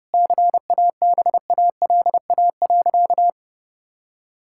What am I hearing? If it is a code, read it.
Morse code, heard as a keyed tone, about 30 wpm, 700 Hz. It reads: CABALA.
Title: Pangram Challenge v2 - 30wpm